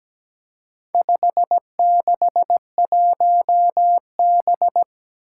Morse code, heard as a keyed tone, about 17 wpm, 700 Hz. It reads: 561B
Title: Four Challenging Letter-Number Combinations 17wpm